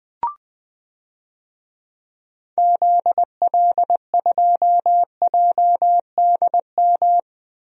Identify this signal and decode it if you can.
Morse code, heard as a keyed tone, about 20 wpm, 700 Hz. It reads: ZL2JDM
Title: DXCC-100 Call Signs - 6 Characters - 20wpm